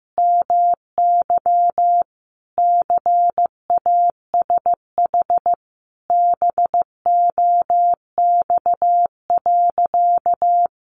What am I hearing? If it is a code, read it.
Morse code, heard as a keyed tone, about 15 wpm, 700 Hz. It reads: MY CASH BOX.
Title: Sentences from Top 700 Words 15wpm